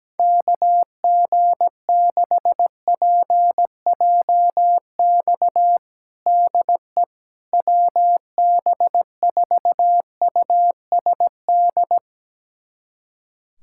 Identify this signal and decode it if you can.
Morse code, heard as a keyed tone, about 17 wpm, 700 Hz. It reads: KG6PJX DE WB4USD
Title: USA Callsign DE Callsign - 6 Characters - 17wpm